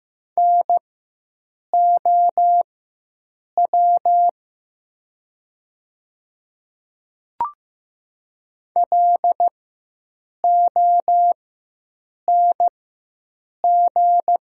Morse code, heard as a keyed tone, about 15 wpm, spaced as for 7 wpm, 700 Hz. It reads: NOW LONG
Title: 4x Character Spacing - Top 100 Words - 15wpm